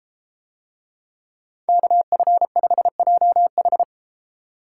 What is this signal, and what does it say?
Morse code, heard as a keyed tone, about 33 wpm, 700 Hz. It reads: KF5JH